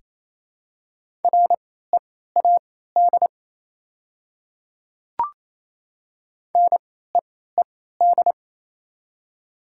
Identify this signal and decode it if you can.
Morse code, heard as a keyed tone, about 28 wpm, spaced as for 16 wpm, 700 Hz. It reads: READ NEED
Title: 3x Character Spacing - Top 300 Words - 28wpm